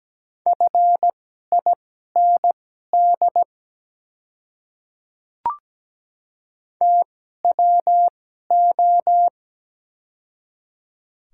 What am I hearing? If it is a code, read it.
Morse code, heard as a keyed tone, about 17 wpm, spaced as for 12 wpm, 700 Hz. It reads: FIND TWO
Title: 2x Character Spacing - Top 100 Words - 17wpm